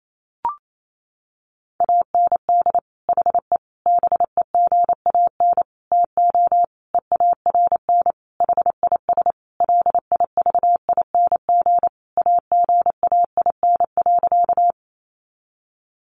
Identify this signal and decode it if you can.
Morse code, heard as a keyed tone, about 28 wpm, 700 Hz. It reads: AND HE BEGAN TO EARN HIS LIVING AGAIN.
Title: Sentences from Top 1500 Words 28wpm